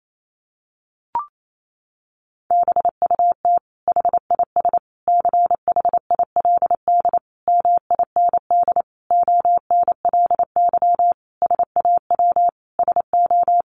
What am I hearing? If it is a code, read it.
Morse code, heard as a keyed tone, about 28 wpm, 700 Hz. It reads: BUT HIS CHILD MIND ONLY SAW SO
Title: Sentences from Top 800 Words 28wpm